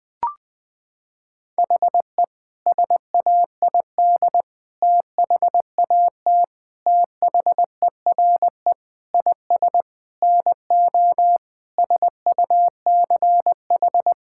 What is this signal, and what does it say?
Morse code, heard as a keyed tone, about 21 wpm, 700 Hz. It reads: HE SAID THAT THERE IS NO SUCH
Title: Sentences with 8-Word Phrases - 20wpm